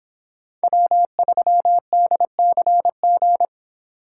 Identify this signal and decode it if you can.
Morse code, heard as a keyed tone, about 25 wpm, 700 Hz. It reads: W3DCG